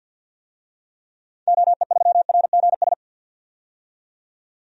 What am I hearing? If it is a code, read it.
Morse code, heard as a keyed tone, about 50 wpm, 700 Hz. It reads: OE3RGS
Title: DXCC-100 Call Signs - 6 Characters - 50wpm